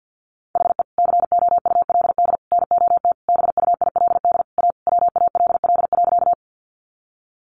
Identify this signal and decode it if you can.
Morse code, heard as a keyed tone, about 50 wpm, 700 Hz. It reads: HE COULD NOT BUILD A WALL.
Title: Sentences from Top 1100 Words 50wpm